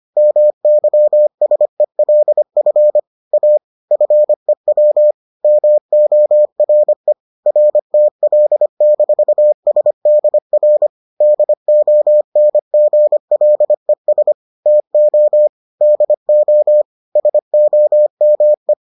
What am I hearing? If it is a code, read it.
Morse code, heard as a keyed tone, about 25 wpm, 600 Hz. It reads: MYSELF A FEW MORE RTL-SDR DONGLES TO DO SOME